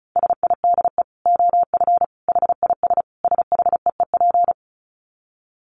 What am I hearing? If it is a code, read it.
Morse code, heard as a keyed tone, about 34 wpm, 700 Hz. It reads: SIDE OF HIS SHEEP
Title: Dit Challenge 35wpm